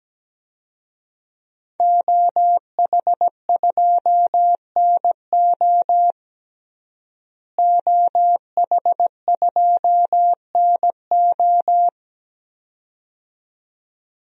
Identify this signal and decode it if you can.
Morse code, heard as a keyed tone, about 17 wpm, 700 Hz. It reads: OH2NO OH2NO
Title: DXCC-100 Call Signs - 5 Characters - Repeated 2x - 17wpm